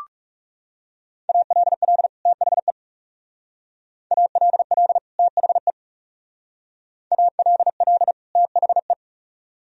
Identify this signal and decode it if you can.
Morse code, heard as a keyed tone, about 38 wpm, 700 Hz. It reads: ALL THE ALL THE ALL THE